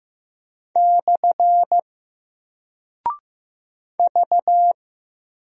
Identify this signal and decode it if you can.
Morse code, heard as a keyed tone, about 15 wpm, 700 Hz. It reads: / V
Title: Single Character (Letter:Number:Punctuation:Prosign) Alt 15wpm